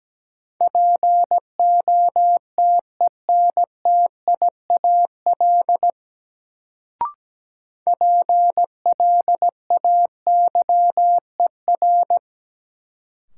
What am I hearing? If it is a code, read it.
Morse code, heard as a keyed tone, about 17 wpm, 700 Hz. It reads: POTENTIAL PLAYER
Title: Sentences from Top 1000 Words - Review Part 4 - New Word Drill - 17wpm